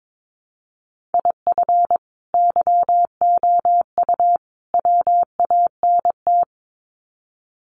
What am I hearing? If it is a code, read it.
Morse code, heard as a keyed tone, about 22 wpm, 700 Hz. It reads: IF YOU WANT
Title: Sets of 3 Words 22wpm